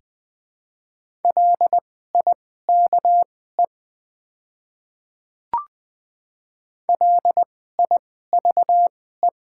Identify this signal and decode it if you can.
Morse code, heard as a keyed tone, about 20 wpm, spaced as for 14 wpm, 700 Hz. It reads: LIKE LIVE
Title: 2x Character Spacing - Top 300 Words - 20wpm